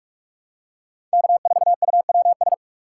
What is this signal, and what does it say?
Morse code, heard as a keyed tone, about 46 wpm, 700 Hz. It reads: K4UWS
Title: USA Call Signs - 5 Characters - 2x Character Spacing - 45wpm